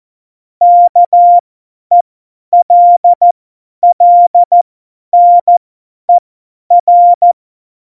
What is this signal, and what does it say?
Morse code, heard as a keyed tone, about 14 wpm, spaced as for 10 wpm, 700 Hz. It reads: KELLNER
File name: Morsekurs / Witze / Witz 07. Januar 2020, 50 F70